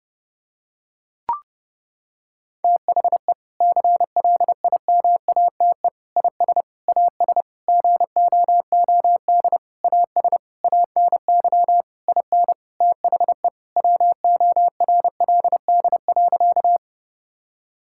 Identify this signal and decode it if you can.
Morse code, heard as a keyed tone, about 30 wpm, 700 Hz. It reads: THE CLIMATE IS AS GOOD AS ANY IN THE WORLD.
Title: Sentences from Top 1600 Words 30wpm